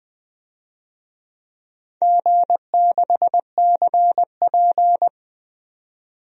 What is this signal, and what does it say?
Morse code, heard as a keyed tone, about 20 wpm, 700 Hz. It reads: G6CP
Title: DXCC-100 Call Signs - 4 Characters - 20wpm